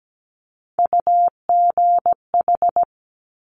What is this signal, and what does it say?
Morse code, heard as a keyed tone, about 17 wpm, 700 Hz. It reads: UGH
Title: Top Three Letter Combinations in Words 17wpm